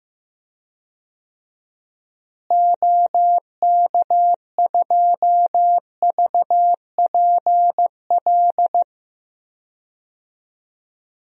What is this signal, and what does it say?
Morse code, heard as a keyed tone, about 15 wpm, 700 Hz. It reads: OK2VPL
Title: DXCC-100 Call Signs - 6 Characters - Repeated 2x - 15wpm